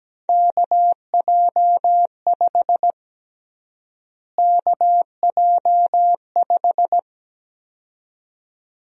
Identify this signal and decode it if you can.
Morse code, heard as a keyed tone, about 17 wpm, 700 Hz. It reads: KJ5 KJ5
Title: USA Call Sign Prefixes - Repeated 2x - 17wpm